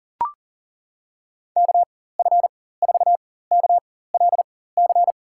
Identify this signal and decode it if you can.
Morse code, heard as a keyed tone, about 37 wpm, spaced as for 18 wpm, 700 Hz. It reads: KF4KLC
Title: USA Call Signs - 6 Characters - 4x Character Spacing - 40wpm